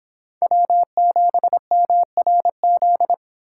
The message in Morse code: W7MRZ

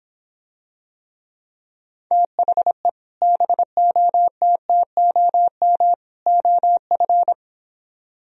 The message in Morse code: THE BOTTOM OF

